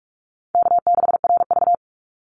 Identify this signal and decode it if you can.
Morse code, heard as a keyed tone, about 44 wpm, 700 Hz. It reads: K6RV
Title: Call Signs Repeated 2x - 45wpm